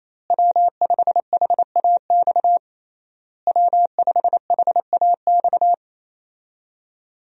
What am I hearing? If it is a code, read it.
Morse code, heard as a keyed tone, about 27 wpm, 700 Hz. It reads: W5HAX W5HAX
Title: USA Call Signs - 5 Characters - Repeated 3x - 28wpm